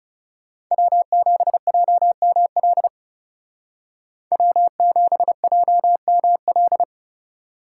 Morse code, heard as a keyed tone, about 39 wpm, 700 Hz. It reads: W7JML W7JML